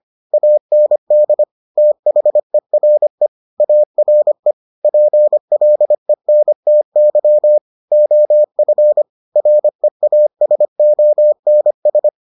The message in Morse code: AND THERE ARE PLENTY OF REASONS

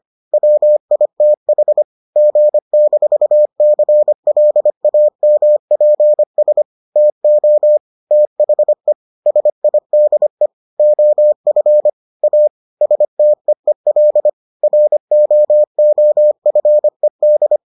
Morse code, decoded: WITH G-CLAMPS TO THE SIDE OF A STEEL ROOFED

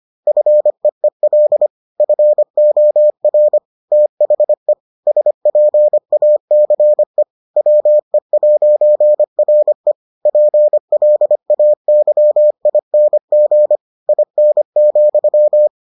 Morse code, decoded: FEEL FOR THE SPACE WE'RE PLAYING IN,

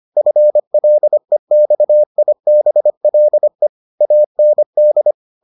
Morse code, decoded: FLEXIBLE AND

